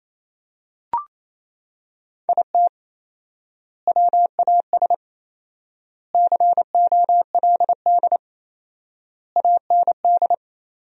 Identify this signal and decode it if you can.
Morse code, heard as a keyed tone, about 28 wpm, 700 Hz. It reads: IT WAS COLD AND
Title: 4x Word-Spacing - Sentences from Top 500 Words - 28wpm